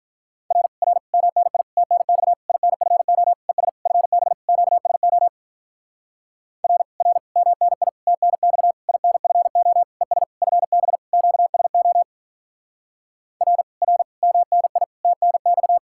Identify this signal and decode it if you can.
Morse code, heard as a keyed tone, about 45 wpm, 700 Hz. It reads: R R MNI TNX INFO ES FB QSO R R MNI TNX INFO ES FB QSO R R MNI TNX